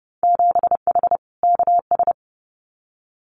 Morse code, heard as a keyed tone, about 30 wpm, 700 Hz. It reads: 7H KS
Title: Winter Field Day Exchanges Encore 30wpm